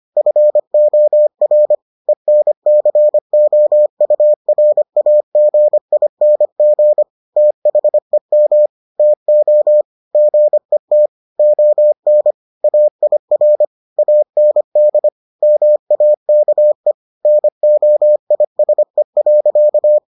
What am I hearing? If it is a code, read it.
Morse code, heard as a keyed tone, about 25 wpm, 600 Hz. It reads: FOR ENCOURAGING THEM TO GET ON AIR AND MAKE NOISE.